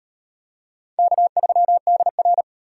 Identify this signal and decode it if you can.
Morse code, heard as a keyed tone, about 38 wpm, 700 Hz. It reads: K3DR